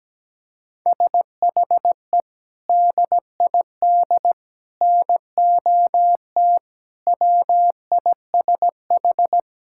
Morse code, heard as a keyed tone, about 17 wpm, 700 Hz. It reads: SHE DID NOT WISH
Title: Sentences from Top 1600 Words 17wpm